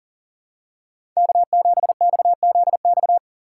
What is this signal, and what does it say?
Morse code, heard as a keyed tone, about 40 wpm, 700 Hz. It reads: K7XZX
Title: USA Call Signs - 5 Characters - 4x Character Spacing - 40wpm